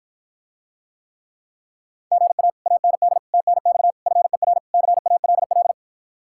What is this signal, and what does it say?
Morse code, heard as a keyed tone, about 54 wpm, 700 Hz. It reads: GA AND TNX FER CALL